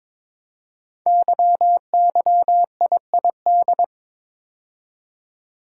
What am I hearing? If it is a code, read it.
Morse code, heard as a keyed tone, about 23 wpm, 700 Hz. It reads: YYIID
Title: Code Groups - 5 Letter:Numbers - 22wpm